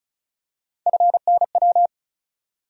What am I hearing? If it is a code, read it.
Morse code, heard as a keyed tone, about 34 wpm, 700 Hz. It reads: FNW
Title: USA Call Sign Postfixes - 3 Letters 35wpm